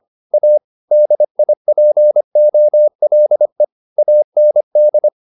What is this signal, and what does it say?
Morse code, heard as a keyed tone, about 25 wpm, 600 Hz. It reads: A DIPOLE AND
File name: SS305 25